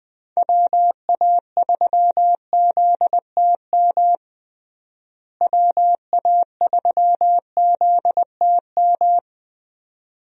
Morse code, heard as a keyed tone, about 20 wpm, 700 Hz. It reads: WA3ZTM WA3ZTM
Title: USA Call Signs - 6 Characters - Repeated 2x - 20wpm